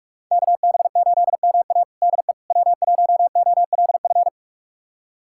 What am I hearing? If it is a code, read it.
Morse code, heard as a keyed tone, about 46 wpm, 700 Hz. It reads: KB8MA DE W1OLF
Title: USA Callsign DE Callsign - 5 Characters - 45wpm